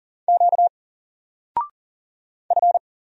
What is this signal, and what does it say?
Morse code, heard as a keyed tone, about 39 wpm, 700 Hz. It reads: Q F
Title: Single Letter-Number Alt 40wpm